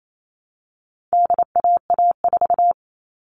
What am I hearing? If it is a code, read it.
Morse code, heard as a keyed tone, about 28 wpm, 700 Hz. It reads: DAA4
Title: All Cut Numbers - 4-Digit 28wpm